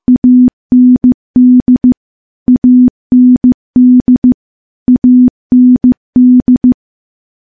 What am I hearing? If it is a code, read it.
Morse code, heard as a keyed tone, about 15 wpm, 250 Hz. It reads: AND AND AND
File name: morse-training-15WPM